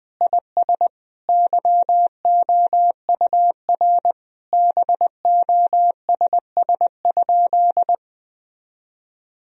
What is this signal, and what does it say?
Morse code, heard as a keyed tone, about 20 wpm, 700 Hz. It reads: IS YOUR BOSS?
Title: Sentences from Top 1000 Words - Review Part 4 - 20wpm